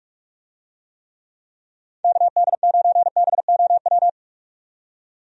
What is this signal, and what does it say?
Morse code, heard as a keyed tone, about 46 wpm, 700 Hz. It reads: KD9BOW
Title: USA Call Signs - 6 Characters - 45wpm